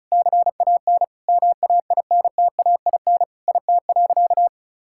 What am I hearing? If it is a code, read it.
Morse code, heard as a keyed tone, about 36 wpm, 700 Hz. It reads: CAN MAINTAIN IT.
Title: Sentences from Top 1000 Words - Review Part 5 - 35wpm